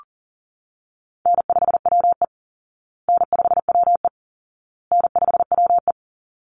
Morse code, heard as a keyed tone, about 40 wpm, 700 Hz. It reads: N5WE N5WE N5WE
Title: Easy Call Signs Repeated 3x - 40wpm